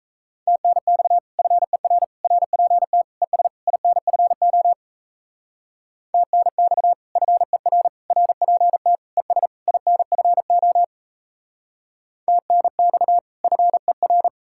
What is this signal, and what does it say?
Morse code, heard as a keyed tone, about 34 wpm, 700 Hz. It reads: TNX FER RPT ES INFO TNX FER RPT ES INFO TNX FER
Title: Speed-Racing Common QSO Phrases Encore - 28wpm